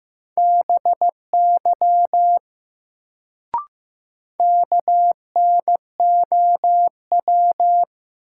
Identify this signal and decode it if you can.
Morse code, heard as a keyed tone, about 15 wpm, 700 Hz. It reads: BY KNOW
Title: Sentences from Top 100 Words - New Word Drill - 15wpm